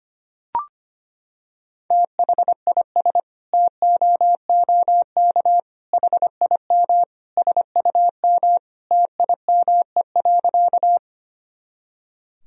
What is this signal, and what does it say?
Morse code, heard as a keyed tone, about 25 wpm, 700 Hz. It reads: THIS TOOK HIM SUM TIME.
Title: Sentences from Top 300 Words and Abbreviations 25wpm